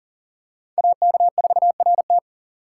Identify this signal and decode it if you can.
Morse code, heard as a keyed tone, about 39 wpm, 700 Hz. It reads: AK4RT
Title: USA Call Signs - 5 Characters - 4x Character Spacing - 40wpm